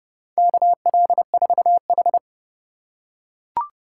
Morse code, heard as a keyed tone, about 32 wpm, 700 Hz. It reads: KL4H
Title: USA Call Signs - 4 Characters - 2x Character Spacing - 30wpm